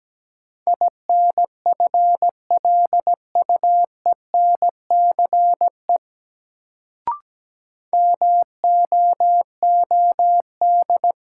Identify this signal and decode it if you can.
Morse code, heard as a keyed tone, about 17 wpm, 700 Hz. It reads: INFLUENCE MOOD
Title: Sentences from Top 1100 Words - New Word Drill Alt - 17wpm